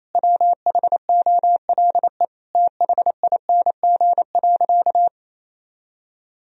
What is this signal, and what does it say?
Morse code, heard as a keyed tone, about 28 wpm, 700 Hz. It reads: WHOLE THING.
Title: Sentences from Top 1000 Words - Review Part 1 - 28wpm